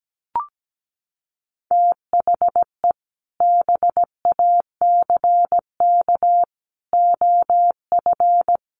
Morse code, heard as a keyed tone, about 17 wpm, 700 Hz. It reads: THE BACK OF
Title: Sets of 5 Words Encore2 17wpm